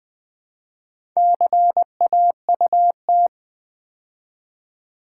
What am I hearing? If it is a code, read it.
Morse code, heard as a keyed tone, about 20 wpm, 700 Hz. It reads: CAUT
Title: Build A Word - 2 Letters At A Time - 8-Letter Words - 20wpm